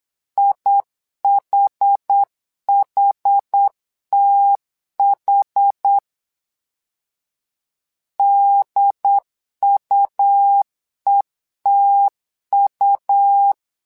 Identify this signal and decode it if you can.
Morse code, heard as a keyed tone, about 8 wpm, 800 Hz. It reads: IHHTH DUETU
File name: LEZIONI / 2 UHDN / 2 UHD 8wpm 2258